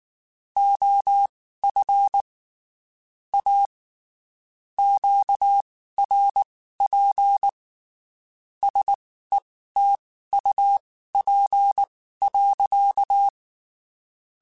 Morse code, heard as a keyed tone, about 19 wpm, spaced as for 14 wpm, 800 Hz. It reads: OF A QRP SETUP.
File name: (spaced) / QSOs Spaced-mp3s / WD0BGP 1 @18 sp